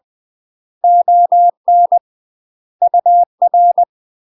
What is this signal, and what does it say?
Morse code, heard as a keyed tone, about 20 wpm, 700 Hz. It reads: ON UR
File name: UK MASTER QSOs TRAINING / 20WPM / UK MASTER QSOs 0048